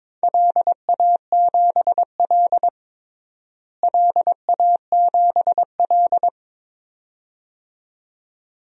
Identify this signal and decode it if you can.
Morse code, heard as a keyed tone, about 22 wpm, 700 Hz. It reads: LA7L LA7L
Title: DXCC-100 Call Signs - 4 Characters - Repeated 2x - 22wpm